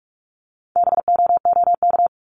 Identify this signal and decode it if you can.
Morse code, heard as a keyed tone, about 44 wpm, 700 Hz. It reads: BOOK
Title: ES:S Suffix - Top 100 Words with Base Words 45wpm